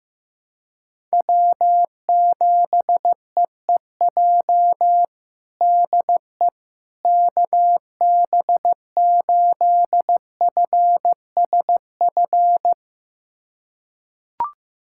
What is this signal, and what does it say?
Morse code, heard as a keyed tone, about 15 wpm, 700 Hz. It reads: W7EEJ DE KB8FSF
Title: USA Callsign DE Callsign 15wpm